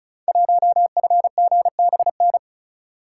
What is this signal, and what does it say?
Morse code, heard as a keyed tone, about 36 wpm, 700 Hz. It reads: 1FGBN
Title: Code Groups - 5 Letter:Numbers - 35wpm